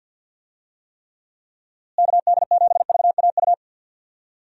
Morse code, heard as a keyed tone, about 49 wpm, 700 Hz. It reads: KD7VAU
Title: USA Call Signs - 6 Characters - Repeated 2x - 50wpm